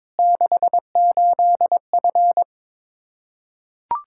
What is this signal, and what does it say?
Morse code, heard as a keyed tone, about 22 wpm, 700 Hz. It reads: 68F
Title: Practice QSO - Exchange 2 - Info - 22wpm